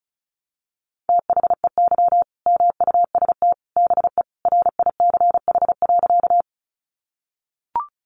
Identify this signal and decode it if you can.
Morse code, heard as a keyed tone, about 34 wpm, 700 Hz. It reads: THEY MUST BE RICH.
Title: Sentences from Top 1100 Words 35wpm